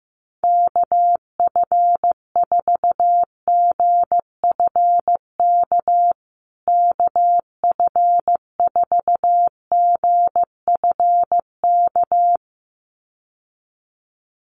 Morse code, heard as a keyed tone, about 15 wpm, 700 Hz. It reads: KF4GFK KF4GFK
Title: USA Call Signs Repeated 2x 15wpm